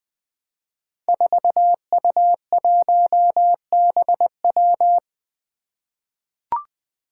Morse code, heard as a keyed tone, about 20 wpm, 700 Hz. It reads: 4U1BW
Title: Five Challenging Letter-Number Combinations 20wpm